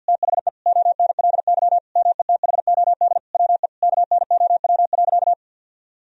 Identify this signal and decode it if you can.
Morse code, heard as a keyed tone, about 50 wpm, 700 Hz. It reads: THE ONLY METHOD WE KNOW.
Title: Sentences from Top 1000 Words - Review Part 3 - 50wpm